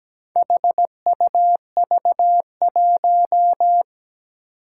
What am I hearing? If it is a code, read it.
Morse code, heard as a keyed tone, about 17 wpm, 700 Hz. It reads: HUV1
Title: Four Challenging Letter-Number Combinations 17wpm